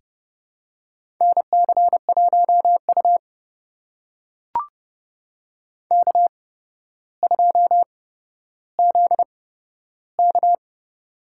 Morse code, heard as a keyed tone, about 30 wpm, spaced as for 13 wpm, 700 Hz. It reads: NC1U K2ZK
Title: USA Call Signs - 4 Characters - 8x Character Spacing - 30wpm